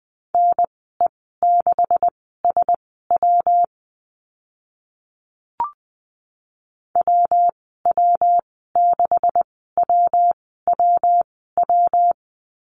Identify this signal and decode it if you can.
Morse code, heard as a keyed tone, about 20 wpm, spaced as for 15 wpm, 700 Hz. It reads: NE6SW WW6WWW
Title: 140 Lesson 23g - Letter f - Call Signs 20wpm